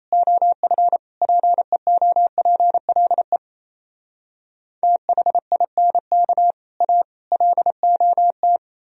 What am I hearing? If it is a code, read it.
Morse code, heard as a keyed tone, about 37 wpm, 700 Hz. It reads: OF PEOPLE THINK A LOT